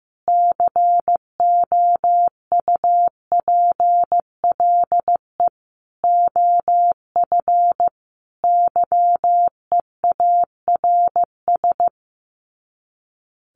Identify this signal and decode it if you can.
Morse code, heard as a keyed tone, about 15 wpm, 700 Hz. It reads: COUPLE OF YEARS
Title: Sets of 5 Words Encore 15wpm